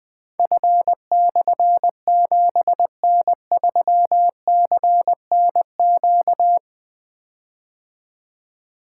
Morse code, heard as a keyed tone, about 20 wpm, 700 Hz. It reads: F/7N3CNQ
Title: Difficult Call Signs 20wpm